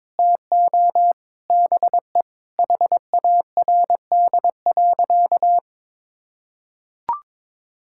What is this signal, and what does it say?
Morse code, heard as a keyed tone, about 21 wpm, 700 Hz. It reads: TO BE HARD.
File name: practice / 22wpm / Sentences with 4-Word Phrases - 22wpm